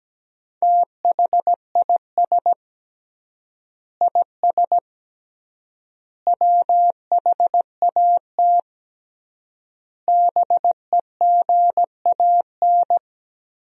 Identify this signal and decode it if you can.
Morse code, heard as a keyed tone, about 17 wpm, 700 Hz. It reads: THIS IS WHAT BEGAN
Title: 3x Word-Spacing - Sentences from Top 300 Words - 17wpm